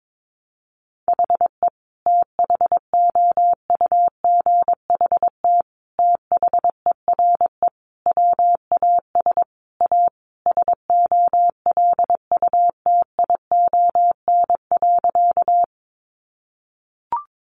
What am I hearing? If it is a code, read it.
Morse code, heard as a keyed tone, about 22 wpm, 700 Hz. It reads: HE THOUGHT THERE WAS A SOLUTION.
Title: Sentences from Top 1300 Words 22wpm